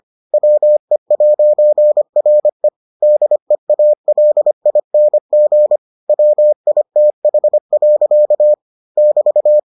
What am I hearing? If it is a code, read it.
Morse code, heard as a keyed tone, about 25 wpm, 600 Hz. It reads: WE'RE DEALING WITH. <BT>